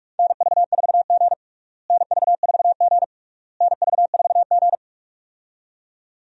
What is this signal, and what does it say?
Morse code, heard as a keyed tone, about 46 wpm, 700 Hz. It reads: NV4G NV4G NV4G
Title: USA Call Signs - 4 Characters - Repeated 3x - 45wpm